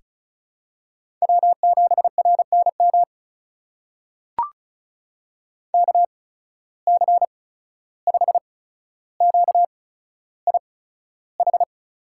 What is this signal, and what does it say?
Morse code, heard as a keyed tone, about 35 wpm, spaced as for 14 wpm, 700 Hz. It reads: W7RNM KC5QIH
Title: USA Call Signs - 8x Character Spacing - 35wpm